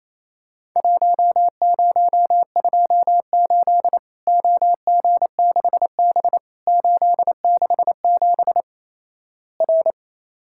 Morse code, heard as a keyed tone, about 28 wpm, 700 Hz. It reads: 1028 OG6B 867 R
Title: Worked All Europe DX Contest - QTC Exchange - 28wpm